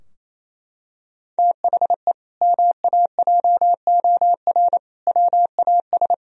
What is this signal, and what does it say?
Morse code, heard as a keyed tone, about 28 wpm, 700 Hz. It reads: THE MAJOR WAS
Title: Sentences from Top 1000 Words - Review Part 4 - 28wpm